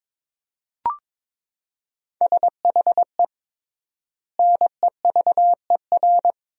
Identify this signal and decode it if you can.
Morse code, heard as a keyed tone, about 22 wpm, 700 Hz. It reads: SHE NEVER
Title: 3x Word-Spacing - Sentences from Top 300 Words - 22wpm